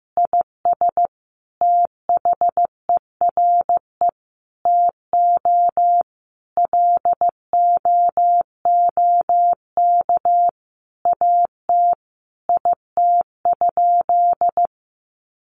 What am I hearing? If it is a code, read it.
Morse code, heard as a keyed tone, about 15 wpm, 700 Hz. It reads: IS THERE TO LOOK AT IT?
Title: Sentences from Top 100 Words 15wpm